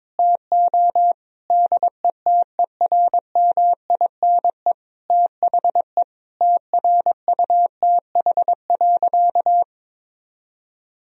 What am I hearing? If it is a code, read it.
Morse code, heard as a keyed tone, about 21 wpm, 700 Hz. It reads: TO DETERMINE THE TRUTH.
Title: Sentences from Top 1000 Words - Review Part 4 - 22wpm